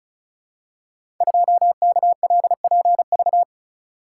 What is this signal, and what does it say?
Morse code, heard as a keyed tone, about 36 wpm, 700 Hz. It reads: 2KLPV
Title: Code Groups - 5 Letter:Numbers - 35wpm